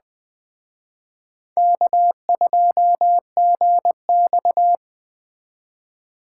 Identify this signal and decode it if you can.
Morse code, heard as a keyed tone, about 20 wpm, 700 Hz. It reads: K2GX
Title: Speed Racing - USA 1x2 Call Signs 17wpm